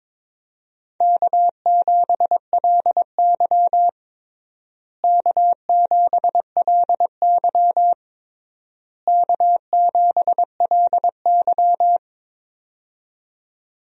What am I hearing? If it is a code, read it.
Morse code, heard as a keyed tone, about 22 wpm, 700 Hz. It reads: K7LY K7LY K7LY
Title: USA Call Signs - 4 Characters - Repeated 3x - 22wpm